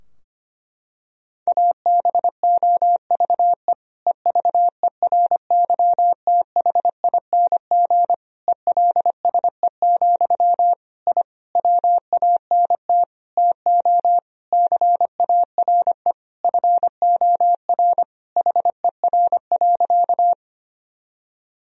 Morse code, heard as a keyed tone, about 25 wpm, 700 Hz. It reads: ABOVE EVERYTHING ELSE, I WANT TO CARE FOR HER.
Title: Sentences from Top 1000 Words - Review Part 2 - 25wpm